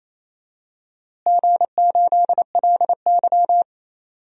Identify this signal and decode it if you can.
Morse code, heard as a keyed tone, about 28 wpm, 700 Hz. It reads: G8LY
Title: DXCC-100 Call Signs - 4 Characters - Repeated 3x - 28wpm